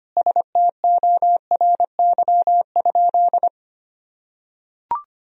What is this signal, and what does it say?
Morse code, heard as a keyed tone, about 25 wpm, 700 Hz. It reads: STORY?
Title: Sentences from Top 1000 Words - Review Part 1 - 25wpm